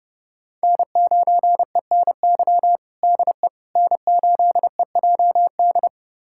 Morse code, heard as a keyed tone, about 30 wpm, 700 Hz. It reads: N9ENY DE N8EJD
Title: USA Callsign DE Callsign - 5 Characters - 30wpm